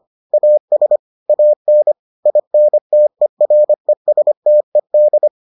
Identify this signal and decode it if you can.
Morse code, heard as a keyed tone, about 25 wpm, 600 Hz. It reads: AS AN INTERESTED